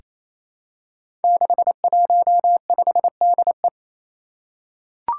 Morse code, heard as a keyed tone, about 28 wpm, 700 Hz. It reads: 615DE
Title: Five Challenging Letter-Number Combinations 28wpm